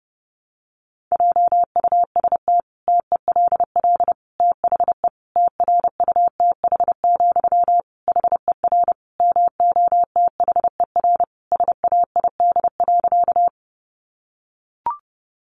Morse code, heard as a keyed tone, about 30 wpm, 700 Hz. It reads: JUST TELL THE TRUTH, HER MOTHER SAID.
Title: Sentences from Top 800 Words 30wpm